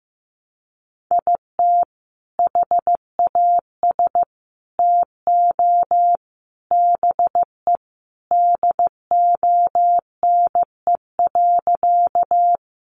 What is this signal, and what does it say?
Morse code, heard as a keyed tone, about 15 wpm, 700 Hz. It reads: IT HAS TO BE DONE.
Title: Sentences from Top 500 Words 15wpm